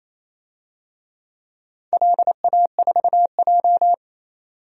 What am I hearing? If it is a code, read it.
Morse code, heard as a keyed tone, about 28 wpm, 700 Hz. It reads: LA4J
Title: DXCC-100 Call Signs - 4 Characters - 28wpm